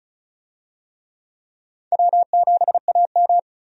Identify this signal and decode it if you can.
Morse code, heard as a keyed tone, about 34 wpm, 700 Hz. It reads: W7AM
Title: USA Call Signs - 4 Characters - 35wpm